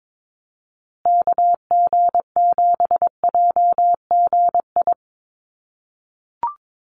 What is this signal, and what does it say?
Morse code, heard as a keyed tone, about 22 wpm, 700 Hz. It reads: KG7JGI
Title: USA Call Signs Repeated 3x 22wpm